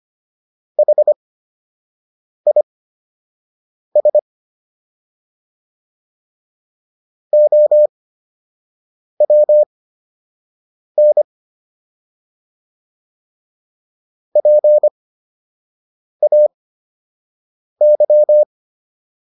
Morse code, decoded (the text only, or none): HIS OWN PAY